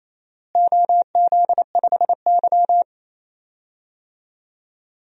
Morse code, heard as a keyed tone, about 28 wpm, 700 Hz. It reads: OZ5Y
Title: DXCC-100 Call Signs - 4 Characters - Repeated 2x - 28wpm